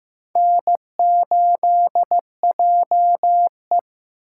Morse code, heard as a keyed tone, about 18 wpm, 700 Hz. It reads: N8JE
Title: Speed Racing - USA 1x2 Call Signs 15wpm